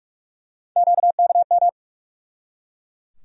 Morse code, heard as a keyed tone, about 45 wpm, 700 Hz. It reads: QKM
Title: USA Call Sign Postfixes - 3 Letters 45wpm